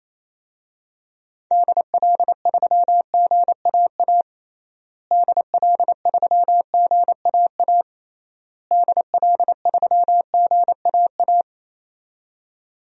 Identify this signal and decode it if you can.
Morse code, heard as a keyed tone, about 28 wpm, 700 Hz. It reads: DL3GAA DL3GAA DL3GAA
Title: DXCC-100 Call Signs - 6 Characters - Repeated 3x - 28wpm